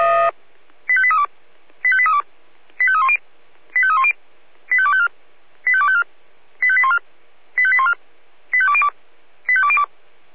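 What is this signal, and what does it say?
Five-tone selective call (ZVEI), heard as quick runs of five tones, tones of about 70 ms. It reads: 85832 85832 85310 85310 85254 85254 85814 85814 85202 85202
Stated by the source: ZVEI five-tone call 85870 received by SDR.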